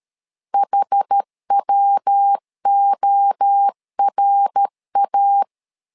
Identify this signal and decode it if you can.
Morse code, heard as a keyed tone, about 13 wpm, 800 Hz. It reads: HWORA
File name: LEZIONI / 4 Q1S 12wpm 1802